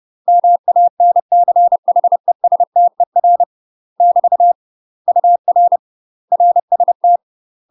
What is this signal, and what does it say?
Morse code, heard as a keyed tone, about 30 wpm, 700 Hz. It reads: MANCHESTER <BT> UR RST
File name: UK MASTER QSOs TRAINING / 30WPM / UK MASTER QSOs 0028